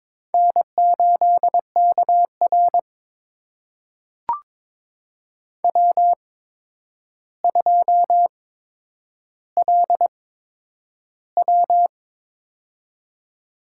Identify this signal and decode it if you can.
Morse code, heard as a keyed tone, about 22 wpm, spaced as for 9 wpm, 700 Hz. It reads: N8KR W2LW
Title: USA Call Signs - 4 Characters - 8x Character Spacing - 22wpm